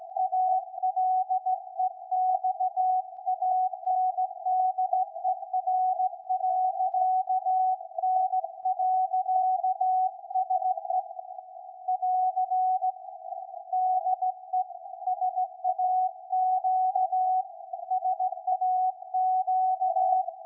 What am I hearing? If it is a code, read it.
Morse code, heard as a keyed tone, about 17 wpm, 740 Hz. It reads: ALEXANDER.N.SE <AR> DE SAQ SAQ
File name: iz7slz / VLF / SAQ 13-02-2023